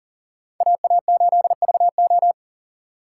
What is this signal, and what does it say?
Morse code, heard as a keyed tone, about 40 wpm, 700 Hz. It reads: AA8VO
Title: USA Call Signs - 5 Characters - 3x Character Spacing - 40wpm